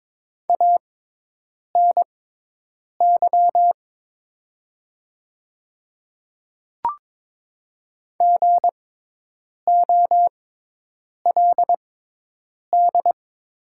Morse code, heard as a keyed tone, about 22 wpm, spaced as for 8 wpm, 700 Hz. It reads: ANY GOLD
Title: 6x Character Spacing - Top 500 Words - 22wpm